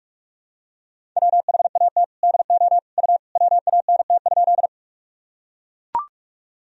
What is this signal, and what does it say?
Morse code, heard as a keyed tone, about 41 wpm, 700 Hz. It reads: WHAT DO U WANT?
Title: Sentences from Top 200 Words and Abbreviations 45wpm